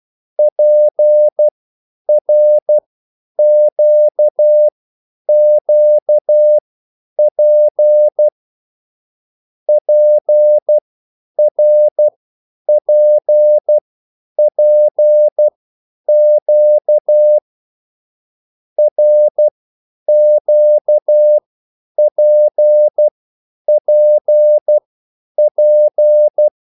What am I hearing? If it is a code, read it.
Morse code, heard as a keyed tone, about 12 wpm, spaced as for 9 wpm, 600 Hz. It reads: PRQQP PRPPQ RQPPP
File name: cwbook